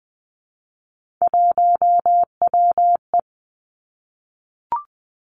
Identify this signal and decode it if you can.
Morse code, heard as a keyed tone, about 20 wpm, 700 Hz. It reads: 1WE
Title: Three Challenging Letter-Number Combinations 20wpm